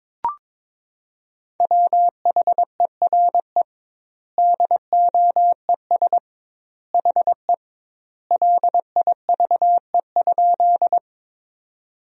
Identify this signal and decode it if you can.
Morse code, heard as a keyed tone, about 22 wpm, 700 Hz. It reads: WHERE DOES HE LIVE?
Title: 2x Word-Spacing - Sentences from Top 200 Words - 22wpm